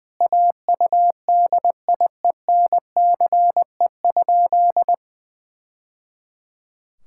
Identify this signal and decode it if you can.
Morse code, heard as a keyed tone, about 21 wpm, 700 Hz. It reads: AUDIENCE?
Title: Sentences from Top 1000 Words - Review Part 5 - 20wpm